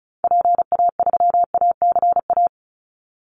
Morse code, heard as a keyed tone, about 34 wpm, 700 Hz. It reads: PA3ACA
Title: Call Signs Repeated 3x - 35wpm